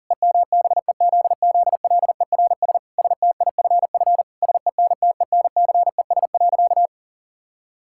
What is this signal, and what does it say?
Morse code, heard as a keyed tone, about 40 wpm, 700 Hz. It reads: EMBEZZLERS STIFF SENTENCES.
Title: Pangram Challenge v2 - 40wpm